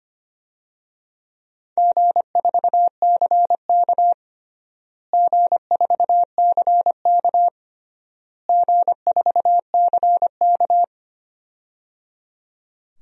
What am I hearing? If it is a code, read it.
Morse code, heard as a keyed tone, about 25 wpm, 700 Hz. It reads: G4CK G4CK G4CK
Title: DXCC-100 Call Signs - 4 Characters - Repeated 3x - 25wpm